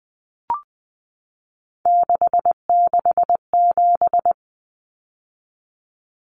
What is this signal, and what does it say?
Morse code, heard as a keyed tone, about 20 wpm, 700 Hz. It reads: 667
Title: Common Cut Numbers - 3-Digit 20wpm